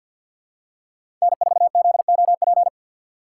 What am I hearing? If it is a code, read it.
Morse code, heard as a keyed tone, about 52 wpm, 690 Hz. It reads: N4ZOP
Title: USA Call Signs - 5 Characters - 6x Character Spacing - 50wpm